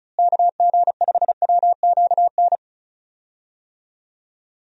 Morse code, heard as a keyed tone, about 36 wpm, 700 Hz. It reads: KG5WQN
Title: USA Call Signs - 6 Characters - Repeated 3x - 35wpm